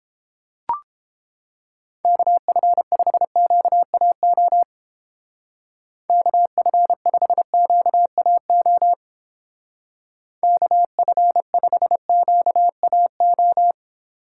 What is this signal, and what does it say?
Morse code, heard as a keyed tone, about 29 wpm, 700 Hz. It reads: KF5QAO KF5QAO KF5QAO